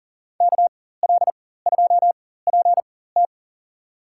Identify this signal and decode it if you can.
Morse code, heard as a keyed tone, about 40 wpm, spaced as for 19 wpm, 700 Hz. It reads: KL2PT
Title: USA Call Signs - 5 Characters - 4x Character Spacing - 40wpm